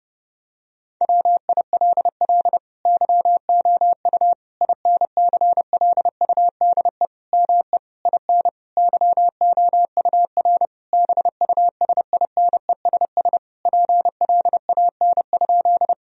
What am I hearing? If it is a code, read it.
Morse code, heard as a keyed tone, about 30 wpm, 700 Hz. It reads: WILL YOU INCLUDE ME IN YOUR BUSINESS PLAN?
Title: Sentences from Top 1000 Words - Review Part 3 - 30wpm